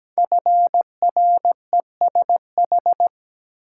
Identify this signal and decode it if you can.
Morse code, heard as a keyed tone, about 17 wpm, 700 Hz. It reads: FRESH
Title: Sentences from Top 1200 Words - New Word Drill Alt - 17wpm